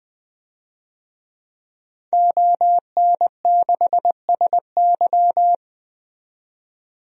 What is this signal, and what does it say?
Morse code, heard as a keyed tone, about 20 wpm, 700 Hz. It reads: ON6SY
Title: DXCC-100 Call Signs - 5 Characters - 20wpm